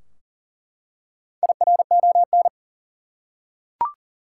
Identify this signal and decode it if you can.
Morse code, heard as a keyed tone, about 40 wpm, 700 Hz. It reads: IRON
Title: Four Letter Words - 40wpm